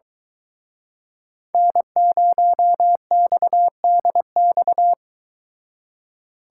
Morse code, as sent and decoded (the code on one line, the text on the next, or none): -. ----- -..- -.. -..-
N0XDX